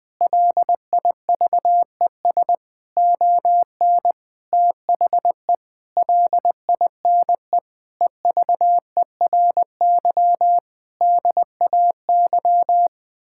Morse code, heard as a keyed tone, about 20 wpm, 700 Hz. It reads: LIVES ON THE LINE EVERY DAY
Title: Sets of 8 Words - 20wpm